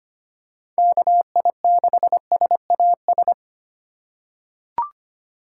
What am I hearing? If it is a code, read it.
Morse code, heard as a keyed tone, about 25 wpm, 700 Hz. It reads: KI6SAS
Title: USA Call Signs - 6 Characters - 4x Character Spacing - 25wpm